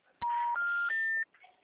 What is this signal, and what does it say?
Special information tone, with noise.
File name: out- -2024- - - .289